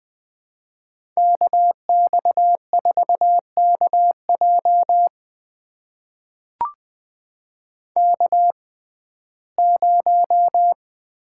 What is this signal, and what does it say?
Morse code, heard as a keyed tone, about 20 wpm, spaced as for 15 wpm, 700 Hz. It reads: KX4KJ K0
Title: USA Call Signs - 5 Characters - 6x Character Spacing - 20wpm